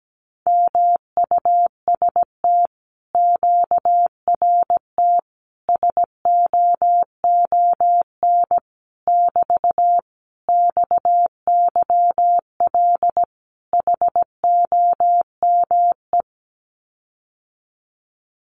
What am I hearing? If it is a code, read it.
Morse code, heard as a keyed tone, about 17 wpm, 700 Hz. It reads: MUST QRT SOON <BT> XYL HOME
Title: Common QSO Phrases Encore 17wpm